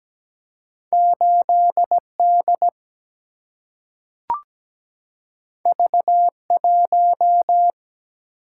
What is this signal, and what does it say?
Morse code, heard as a keyed tone, about 17 wpm, 700 Hz. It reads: 8D V1
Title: Single-Letter Single-Number 17wpm